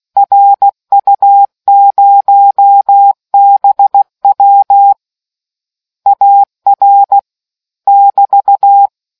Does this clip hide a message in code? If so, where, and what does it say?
Yes: Morse code from 0.2 to 8.9 s, heard as a keyed tone, about 16 wpm, 800 Hz. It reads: RU0BW AR <BT>